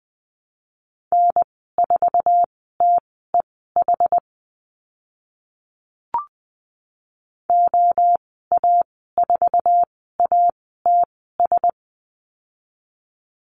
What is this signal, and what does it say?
Morse code, heard as a keyed tone, about 20 wpm, spaced as for 15 wpm, 700 Hz. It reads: N4TEH OA4ATS
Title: 062 Lesson 11g - Letter h - Call Signs 20wpm